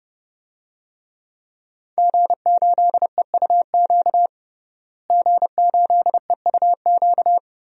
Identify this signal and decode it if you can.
Morse code, heard as a keyed tone, about 30 wpm, 700 Hz. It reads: G8EUQ G8EUQ
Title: DXCC-100 Call Signs - 5 Characters - Repeated 3x - 30wpm